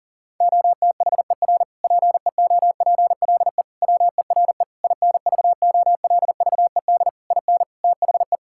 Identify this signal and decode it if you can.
Morse code, heard as a keyed tone, about 40 wpm, 700 Hz. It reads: OTHER PEOPLE WERE INVOLVED IN THE
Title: Sentences from Top 1000 Words - Review Part 5 - 40wpm